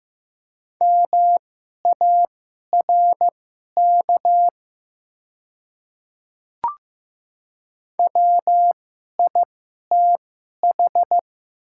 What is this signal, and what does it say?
Morse code, heard as a keyed tone, about 15 wpm, spaced as for 11 wpm, 700 Hz. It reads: MARK WITH
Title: 2x Character Spacing - Top 300 Words - 15wpm